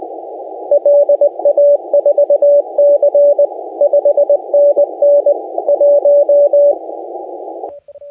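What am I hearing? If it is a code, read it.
Morse code, heard as a keyed tone, about 19 wpm, 560 Hz, with radio noise. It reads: LA4C 5NN 1
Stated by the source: Morse contest audio recorded by LA4C during the SAC CW 2018 contest.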